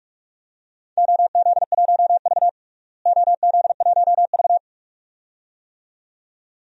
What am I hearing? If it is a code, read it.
Morse code, heard as a keyed tone, about 46 wpm, 700 Hz. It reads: OZ1V OZ1V
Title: DXCC-100 Call Signs - 4 Characters - Repeated 2x - 45wpm